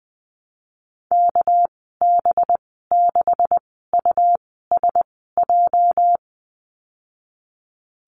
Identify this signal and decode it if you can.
Morse code, heard as a keyed tone, about 20 wpm, spaced as for 15 wpm, 700 Hz. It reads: KB6USJ
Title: 219 Lesson 35g - Number 0 - Call Signs 20wpm